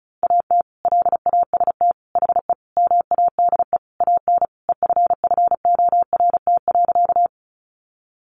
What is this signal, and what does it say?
Morse code, heard as a keyed tone, about 34 wpm, 700 Hz. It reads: AT LAST HE MADE AN EFFORT.
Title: Sentences from Top 1200 Words 35wpm